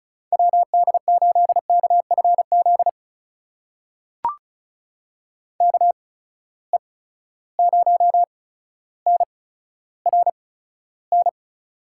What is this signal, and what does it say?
Morse code, heard as a keyed tone, about 36 wpm, spaced as for 15 wpm, 700 Hz. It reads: WD8KFZ KE0NRN
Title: USA Call Signs - 6 Characters - 8x Character Spacing - 35wpm